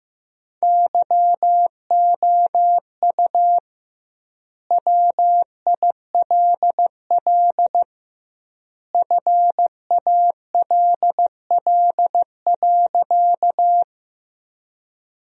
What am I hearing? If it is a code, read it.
Morse code, heard as a keyed tone, about 15 wpm, 700 Hz. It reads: YOU WILL FALL.
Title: 2x Word-Spacing - Sentences from Top 500 Words - 15wpm